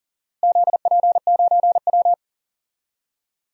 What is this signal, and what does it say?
Morse code, heard as a keyed tone, about 38 wpm, 700 Hz. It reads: ZP9W
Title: DXCC-100 Call Signs - 4 Characters - Repeated 3x - 40wpm